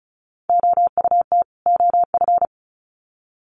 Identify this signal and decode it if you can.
Morse code, heard as a keyed tone, about 34 wpm, 700 Hz. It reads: OUT OF
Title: Sets of 2 Words 35wpm